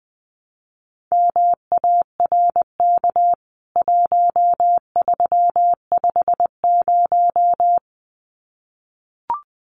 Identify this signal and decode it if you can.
Morse code, heard as a keyed tone, about 20 wpm, 700 Hz. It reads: MARK 1350
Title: CWT Exchange 20wpm